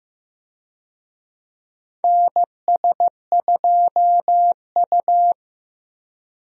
Morse code, heard as a keyed tone, about 15 wpm, 700 Hz. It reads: NS2U
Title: USA Call Signs - 4 Characters - Repeated 3x - 15wpm